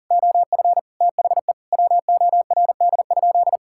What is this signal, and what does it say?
Morse code, heard as a keyed tone, about 40 wpm, 700 Hz. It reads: OF THE WORD?
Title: Sentences from Top 1000 Words - Review Part 6 - 40wpm